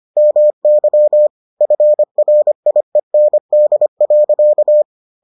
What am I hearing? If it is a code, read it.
Morse code, heard as a keyed tone, about 25 wpm, 600 Hz. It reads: MY FRIEND.